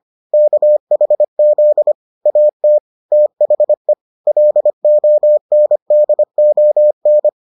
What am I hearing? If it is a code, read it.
Morse code, heard as a keyed tone, about 25 wpm, 600 Hz. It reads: KHZ AT THE LONDON